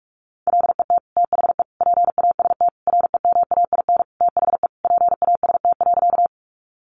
Morse code, heard as a keyed tone, about 44 wpm, 700 Hz. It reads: LET THE PAST REMAIN THE PAST.
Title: Sentences from Top 1200 Words 45wpm